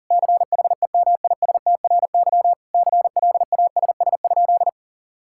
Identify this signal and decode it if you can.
Morse code, heard as a keyed tone, about 40 wpm, 700 Hz. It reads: CHEMISTRY CLASS?
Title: Sentences from Top 1000 Words - Review Part 6 - 40wpm